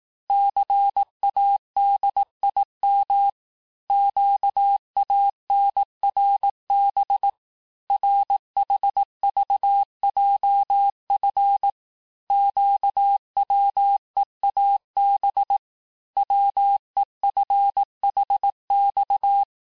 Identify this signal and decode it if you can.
Morse code, heard as a keyed tone, about 18 wpm, 800 Hz. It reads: CADIM QANRB RHVJF QWEAB WEFHX
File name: cours-du-lundi files / cours-1852-texte-chiffre-vitesse-17MPM